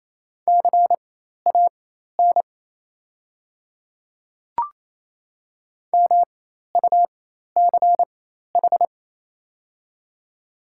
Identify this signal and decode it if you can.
Morse code, heard as a keyed tone, about 28 wpm, spaced as for 13 wpm, 690 Hz. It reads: CAN MUCH
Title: 4x Character Spacing - Top 300 Words - 28wpm